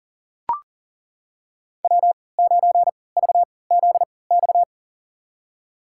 Morse code, heard as a keyed tone, about 40 wpm, spaced as for 23 wpm, 700 Hz. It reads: W9VZX
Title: USA Call Signs - 3x Character Spacing - 40wpm